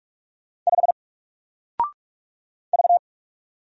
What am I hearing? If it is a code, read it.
Morse code, heard as a keyed tone, about 45 wpm, 700 Hz. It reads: 5 V
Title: Single Letter-Number Alt 45wpm